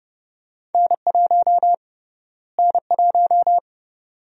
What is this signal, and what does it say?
Morse code, heard as a keyed tone, about 32 wpm, 700 Hz. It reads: N1 N1
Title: USA Call Sign Prefixes - Repeated 2x - 30wpm